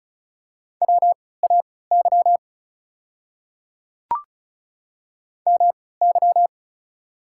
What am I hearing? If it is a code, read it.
Morse code, heard as a keyed tone, about 36 wpm, spaced as for 20 wpm, 690 Hz. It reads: WAY MY
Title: 3x Character Spacing - Top 100 Words - 35wpm